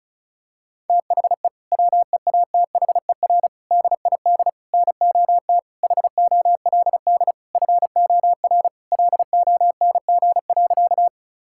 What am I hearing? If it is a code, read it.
Morse code, heard as a keyed tone, about 36 wpm, 700 Hz. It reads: THE WEATHER DID NOT HOLD FOR LONG.
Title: Sentences from Top 1000 Words - Review Part 5 - 35wpm